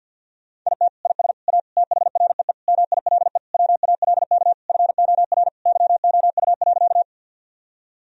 Morse code, heard as a keyed tone, about 48 wpm, 700 Hz. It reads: IT IS A THREE MILE WALK FOR YOU.
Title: Sentences from Top 1000 Words - Review Part 1 - 50wpm